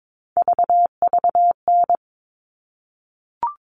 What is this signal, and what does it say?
Morse code, heard as a keyed tone, about 22 wpm, 700 Hz. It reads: VVN
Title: All Cut Numbers - 3-Digit 22wpm